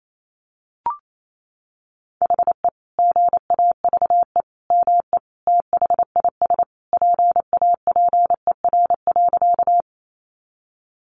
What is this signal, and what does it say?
Morse code, heard as a keyed tone, about 28 wpm, 700 Hz. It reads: HE GAVE ME THIS PAPER.
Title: Sentences from Top 500 Words 28wpm